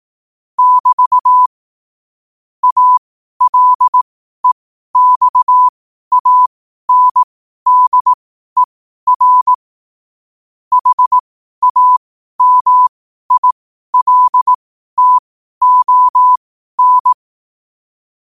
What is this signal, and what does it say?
Morse code, heard as a keyed tone, about 18 wpm, spaced as for 12 wpm, 1000 Hz. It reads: <BT> ALEXANDER HAMILTON
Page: New quotes every day in morse code at 12 Words per minute.